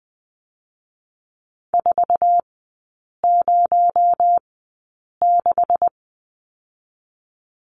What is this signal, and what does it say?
Morse code, heard as a keyed tone, about 20 wpm, spaced as for 8 wpm, 700 Hz. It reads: 406
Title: 218 Lesson 35e - Number 0 - Three Characters 20wpm